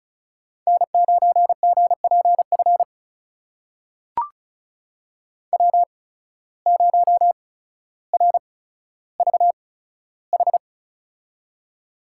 Morse code, heard as a keyed tone, about 36 wpm, spaced as for 15 wpm, 700 Hz. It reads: N9GPF W0RVH
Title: USA Call Signs - 5 Characters - 8x Character Spacing - 35wpm